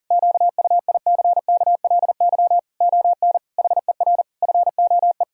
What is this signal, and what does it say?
Morse code, heard as a keyed tone, about 40 wpm, 700 Hz. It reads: QUICKLY ON HER FOE
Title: Pangram Challenge v2 - 40wpm